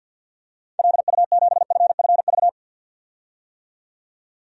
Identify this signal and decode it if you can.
Morse code, heard as a keyed tone, about 52 wpm, 700 Hz. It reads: LU7FFV
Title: DXCC-100 Call Signs - 6 Characters - 50wpm